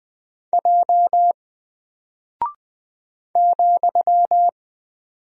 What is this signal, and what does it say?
Morse code, heard as a keyed tone, about 20 wpm, 700 Hz. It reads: J ,
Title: Single Character (Letter:Number:Punctuation:Prosign) Alt 20wpm